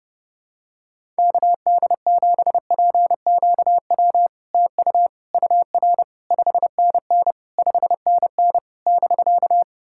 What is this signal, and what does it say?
Morse code, heard as a keyed tone, about 30 wpm, 690 Hz. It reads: KD7PQW TU UR 5NN 5NN <BK>
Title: SOTA QSO Practice 30wpm